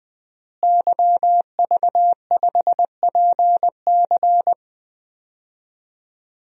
Morse code, heard as a keyed tone, about 21 wpm, 700 Hz. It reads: YV5PC
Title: DXCC-100 Call Signs - 5 Characters - Repeated 3x - 20wpm